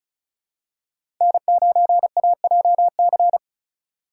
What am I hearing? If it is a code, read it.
Morse code, heard as a keyed tone, about 34 wpm, 700 Hz. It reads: N9AJC
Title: USA Call Signs - 4x Character Spacing - 35wpm